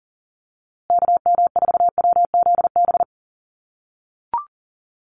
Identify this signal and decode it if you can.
Morse code, heard as a keyed tone, about 39 wpm, 700 Hz. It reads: KM4WZB
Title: USA Call Signs Repeated 3x 40wpm